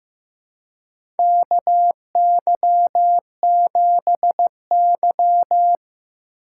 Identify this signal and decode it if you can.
Morse code, heard as a keyed tone, about 15 wpm, 700 Hz. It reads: KY7Y
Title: USA Call Signs - 4 Characters - 2x Character Spacing - 15wpm